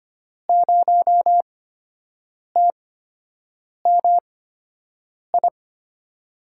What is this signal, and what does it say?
Morse code, heard as a keyed tone, about 25 wpm, spaced as for 7 wpm, 700 Hz. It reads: 0TMI
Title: USA Call Signs - 8x Character Spacing - 25wpm